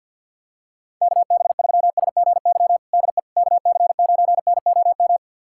Morse code, heard as a keyed tone, about 52 wpm, 700 Hz. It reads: KB3SGY DE KC9NOM
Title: USA Callsign DE Callsign - 6 Characters - 50wpm